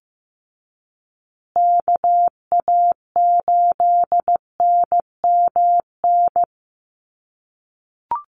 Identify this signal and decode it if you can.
Morse code, heard as a keyed tone, about 15 wpm, 700 Hz. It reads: KA8NMN
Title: USA Call Signs Repeated 3x 15wpm